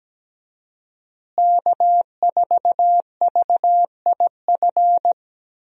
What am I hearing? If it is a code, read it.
Morse code, heard as a keyed tone, about 17 wpm, 700 Hz. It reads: K4VIF
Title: USA Call Signs - 5 Characters - Repeated 2x - 17wpm